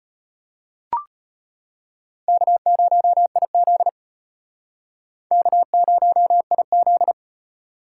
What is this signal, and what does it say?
Morse code, heard as a keyed tone, about 32 wpm, 700 Hz. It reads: K0IZ K0IZ